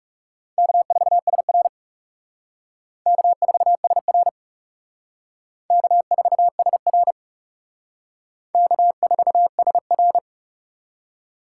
Morse code, heard as a keyed tone, about 37 wpm, 700 Hz. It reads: K4SR K4SR K4SR K4SR
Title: Speed Racing - USA 1x2 Call Signs 30wpm